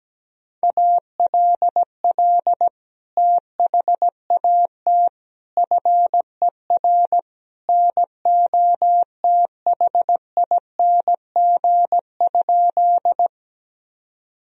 Morse code, heard as a keyed tone, about 17 wpm, 700 Hz. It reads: ALL THAT FER NOTHING?
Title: Sentences from Top 500 Words and Abbreviations 17wpm